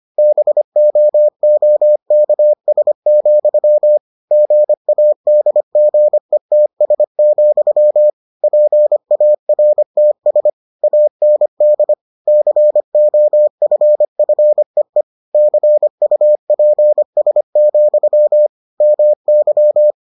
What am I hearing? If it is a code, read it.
Morse code, heard as a keyed tone, about 25 wpm, 600 Hz. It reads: BOOKS, GADGETS, PARTS AND COFFEE CUPS, MY